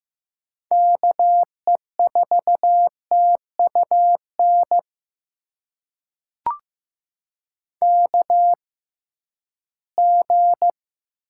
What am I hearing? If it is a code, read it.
Morse code, heard as a keyed tone, about 15 wpm, spaced as for 11 wpm, 700 Hz. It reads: KE4TUN KG
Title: USA Call Signs - 6 Characters - 6x Character Spacing - 15wpm